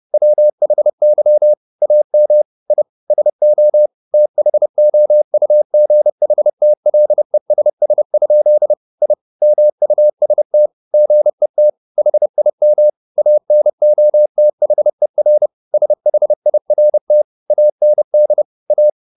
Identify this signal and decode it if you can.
Morse code, heard as a keyed tone, about 30 wpm, 600 Hz. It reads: WHY AM I SO THOUGHTLESS? I MUST GET HIM ANOTHER SHIRT AND A
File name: the old man and the sea / the old man and the sea 1 30wpm